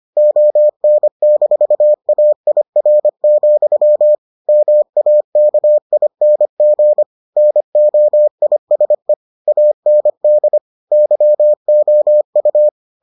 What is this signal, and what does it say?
Morse code, heard as a keyed tone, about 25 wpm, 600 Hz. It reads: ON-AIR, MAKING NOISE AND YOU